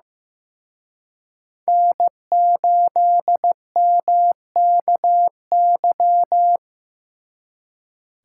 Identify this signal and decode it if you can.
Morse code, heard as a keyed tone, about 18 wpm, 700 Hz. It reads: N8MKY
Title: Speed Racing - USA 1x3 Call Signs 15wpm